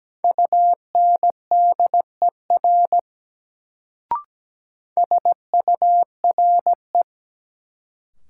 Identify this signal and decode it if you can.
Morse code, heard as a keyed tone, about 17 wpm, 700 Hz. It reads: UNDER SURE
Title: Sentences from Top 300 Words - New Word Drill - 17wpm